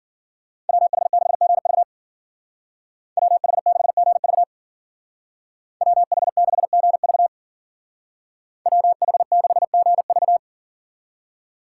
Morse code, heard as a keyed tone, about 49 wpm, 700 Hz. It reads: WH6GV WH6GV WH6GV WH6GV
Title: Speed Racing - USA 2x2 Call Signs 40wpm